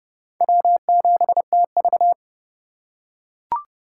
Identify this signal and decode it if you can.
Morse code, heard as a keyed tone, about 28 wpm, 700 Hz. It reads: W7TV